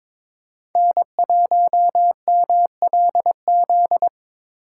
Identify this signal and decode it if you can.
Morse code, heard as a keyed tone, about 26 wpm, 700 Hz. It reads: N1MLZ
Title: Speed Racing - USA Call Signs 22wpm